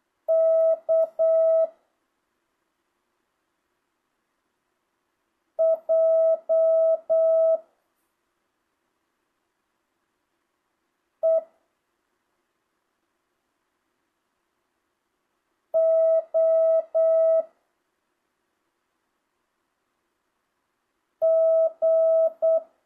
Morse code, heard as a keyed tone, about 8 wpm, 650 Hz. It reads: K J E O G
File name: jh8cbh / 2025cw